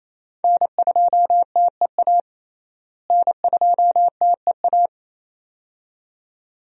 Morse code, heard as a keyed tone, about 28 wpm, 700 Hz. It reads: N2TEA N2TEA
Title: USA Call Signs - 5 Characters - Repeated 2x - 28wpm